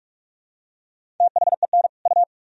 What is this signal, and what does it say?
Morse code, heard as a keyed tone, about 46 wpm, 700 Hz. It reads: THEN U
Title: Sentences from Top 100 Words and Abbreviations 45wpm